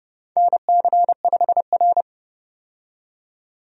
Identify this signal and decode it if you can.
Morse code, heard as a keyed tone, about 30 wpm, 700 Hz. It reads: NC5R
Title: USA Call Signs - 4 Characters - 30wpm